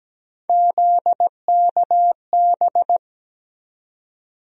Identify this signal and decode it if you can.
Morse code, heard as a keyed tone, about 17 wpm, 700 Hz. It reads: ZKB
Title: USA Call Sign Postfixes - 3 Letters 17wpm